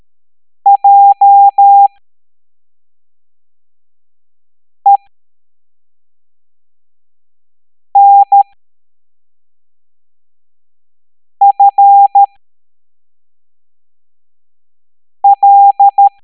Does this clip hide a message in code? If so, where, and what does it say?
Yes: Morse code from 0.7 to 16.1 s, heard as a keyed tone, about 13 wpm, spaced as for 3 wpm, 800 Hz. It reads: JENFL